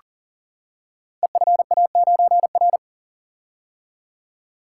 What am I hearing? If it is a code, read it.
Morse code, heard as a keyed tone, about 43 wpm, 690 Hz. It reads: EFA9R
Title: Code Groups - 5 Letter:Numbers - 40wpm